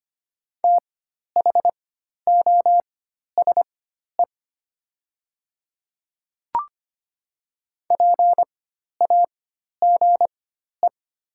Morse code, heard as a keyed tone, about 25 wpm, spaced as for 12 wpm, 700 Hz. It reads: THOSE PAGE
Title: 4x Character Spacing - Top 300 Words - 25wpm